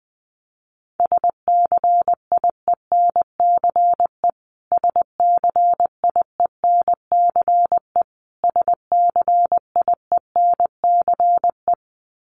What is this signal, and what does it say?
Morse code, heard as a keyed tone, about 20 wpm, 700 Hz. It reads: SCIENCE SCIENCE SCIENCE
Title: Top 500 Words Repeated 20wpm